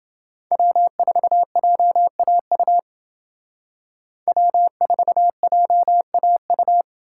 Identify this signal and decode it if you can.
Morse code, heard as a keyed tone, about 28 wpm, 700 Hz. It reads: W4JAU W4JAU